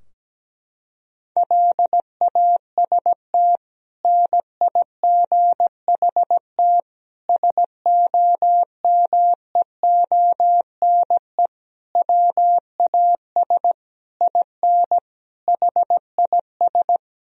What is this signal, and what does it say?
Morse code, heard as a keyed tone, about 17 wpm, 700 Hz. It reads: LAST NIGHT SOMEONE WAS IN HIS
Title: Sentences from Top 1000 Words - Review Part 4 - 17wpm